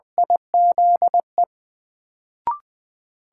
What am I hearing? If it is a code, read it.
Morse code, heard as a keyed tone, about 20 wpm, 700 Hz. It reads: IZE
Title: Sentences from Top 1400 Words - New Word Drill Alt - 20wpm